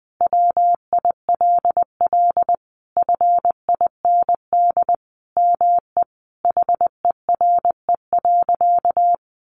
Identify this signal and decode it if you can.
Morse code, heard as a keyed tone, about 20 wpm, 700 Hz. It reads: WILL FIND ME HERE.
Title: Sentences from Top 1200 Words 20wpm